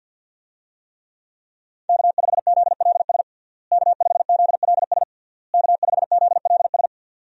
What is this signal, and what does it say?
Morse code, heard as a keyed tone, about 53 wpm, 700 Hz. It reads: K5ZLS K5ZLS K5ZLS
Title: USA Call Signs - 5 Characters - Repeated 3x - 50wpm